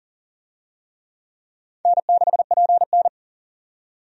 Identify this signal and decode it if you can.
Morse code, heard as a keyed tone, about 40 wpm, 700 Hz. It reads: N6PN
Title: USA Call Signs - 4 Characters - 40wpm